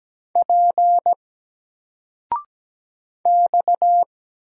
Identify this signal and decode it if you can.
Morse code, heard as a keyed tone, about 17 wpm, 700 Hz. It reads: P X
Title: Single Letter-Number Alt 17wpm